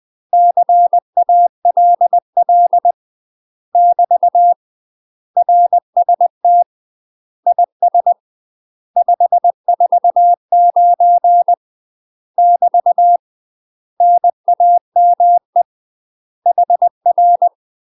Morse code, heard as a keyed tone, about 20 wpm, 700 Hz. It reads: CALL <BT> RST IS 549 <BT> NAME HR
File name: UK MASTER QSOs TRAINING / 20WPM / UK MASTER QSOs 0016